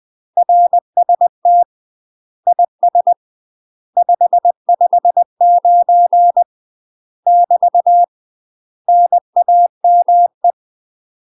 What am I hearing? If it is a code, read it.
Morse code, heard as a keyed tone, about 20 wpm, 700 Hz. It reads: RST IS 559 <BT> NAME
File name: UK MASTER QSOs TRAINING / 20WPM / UK MASTER QSOs 0065